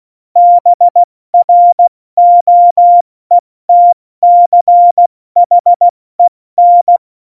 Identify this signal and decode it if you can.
Morse code, heard as a keyed tone, about 16 wpm, 700 Hz. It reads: BROETCHEN
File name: Morsekurs / Witze / Witz 11. September 2018, 70 F80